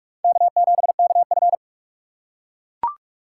Morse code, heard as a keyed tone, about 41 wpm, 700 Hz. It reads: K7KF